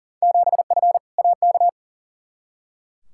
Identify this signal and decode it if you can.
Morse code, heard as a keyed tone, about 40 wpm, 700 Hz. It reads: 7F AK
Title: Field Day Exchanges Encore - 40wpm